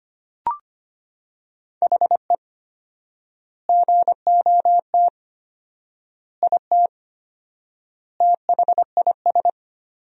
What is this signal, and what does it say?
Morse code, heard as a keyed tone, about 25 wpm, 700 Hz. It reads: HE GOT IT THIS
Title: 4x Word-Spacing - Sentences from Top 400 Words - 25wpm